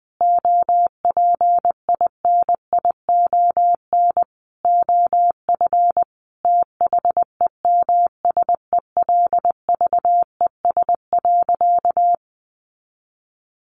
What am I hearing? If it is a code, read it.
Morse code, heard as a keyed tone, about 20 wpm, 700 Hz. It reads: OPINION OF THEMSELVES.
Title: Sentences from Top 1300 Words 20wpm